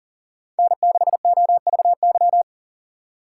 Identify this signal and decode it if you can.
Morse code, heard as a keyed tone, about 40 wpm, 700 Hz. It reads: N6OVY
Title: USA Call Signs - 6x Character Spacing - 40wpm